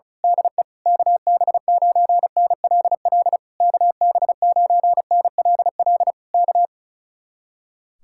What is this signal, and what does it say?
Morse code, heard as a keyed tone, about 32 wpm, 700 Hz. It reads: DE KB9NLL KB9NLL K
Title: Speed-Racing Common QSO Phrases Encore - 35wpm